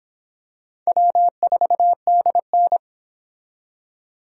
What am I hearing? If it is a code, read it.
Morse code, heard as a keyed tone, about 25 wpm, 700 Hz. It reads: W4DN